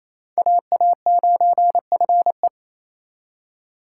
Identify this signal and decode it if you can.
Morse code, heard as a keyed tone, about 35 wpm, 700 Hz. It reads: AA9FE